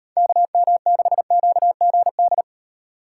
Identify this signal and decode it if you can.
Morse code, heard as a keyed tone, about 30 wpm, 700 Hz. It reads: KM6QGD
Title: Speed Racing - USA 2x3 Call Signs 25wpm